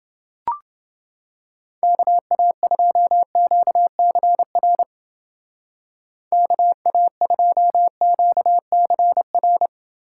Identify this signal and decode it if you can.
Morse code, heard as a keyed tone, about 28 wpm, 700 Hz. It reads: KA2QCR KA2QCR